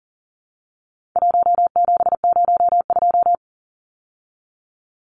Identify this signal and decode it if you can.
Morse code, heard as a keyed tone, about 40 wpm, 700 Hz. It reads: 1702
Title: Numbers - 4-Digit 40wpm